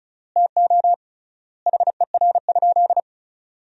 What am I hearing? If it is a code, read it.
Morse code, heard as a keyed tone, about 34 wpm, 700 Hz. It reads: TO HER?
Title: 3x Word-Spacing - Sentences from Top 200 Words - 35wpm